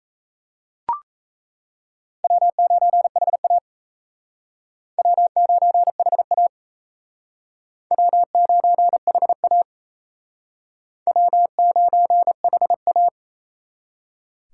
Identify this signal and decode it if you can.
Morse code, heard as a keyed tone, about 35 wpm, 700 Hz. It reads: W9HA W9HA W9HA W9HA